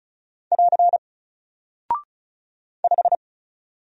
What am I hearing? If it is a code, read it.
Morse code, heard as a keyed tone, about 36 wpm, 700 Hz. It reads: <AR> 5
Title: Single Character (Letter:Number:Punctuation:Prosign) Alt 35wpm